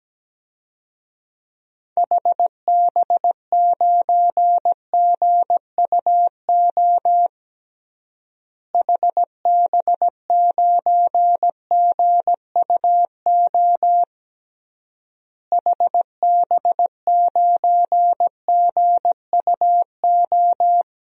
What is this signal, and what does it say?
Morse code, heard as a keyed tone, about 17 wpm, 700 Hz. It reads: HB9GUO HB9GUO HB9GUO
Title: DXCC-100 Call Signs - 6 Characters - Repeated 3x - 17wpm